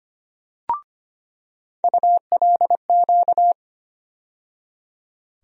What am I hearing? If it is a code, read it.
Morse code, heard as a keyed tone, about 25 wpm, 700 Hz. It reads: ULQ
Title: USA Call Sign Postfixes - 3 Letters 25wpm